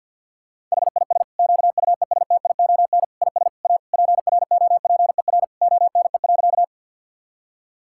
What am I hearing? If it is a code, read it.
Morse code, heard as a keyed tone, about 50 wpm, 700 Hz. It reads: HIS QUESTION IS A PROPER ONE.
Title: Sentences from Top 1000 Words - Review Part 5 - 50wpm